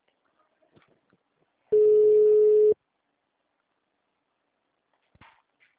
Ringback tone (Europe, 425 Hz).
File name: out- -2053- - - .16693